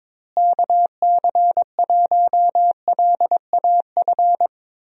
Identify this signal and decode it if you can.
Morse code, heard as a keyed tone, about 22 wpm, 700 Hz. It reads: KC1LAF
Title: USA Call Signs - 6 Characters - Repeated 3x - 22wpm